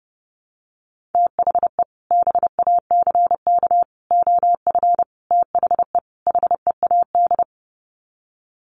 Morse code, heard as a keyed tone, about 30 wpm, 700 Hz. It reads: THE BACK OF THE HEAD
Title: Sets of 5 Words Encore2 30wpm